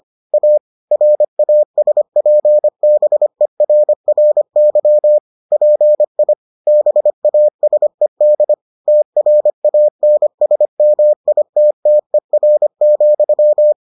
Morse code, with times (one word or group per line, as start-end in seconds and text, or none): 0.3-0.6 A
0.9-5.2 RASPBERRY
5.5-6.3 PI
6.7-8.5 BASED
8.9-13.7 TRANSMITTER,